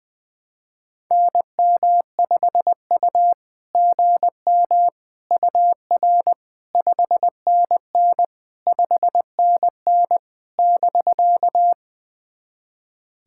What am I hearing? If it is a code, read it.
Morse code, heard as a keyed tone, about 20 wpm, 690 Hz. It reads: NM5U GM UR 5NN 5NN <BK>
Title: SOTA QSO Practice 20wpm